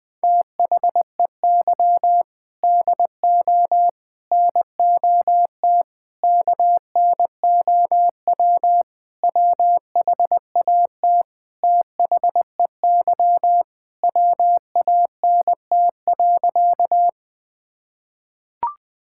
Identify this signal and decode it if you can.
Morse code, heard as a keyed tone, about 21 wpm, 700 Hz. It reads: THEY DO NOT KNOW WHAT THEY WANT.
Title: Sentences with 5-Word Phrases - 20wpm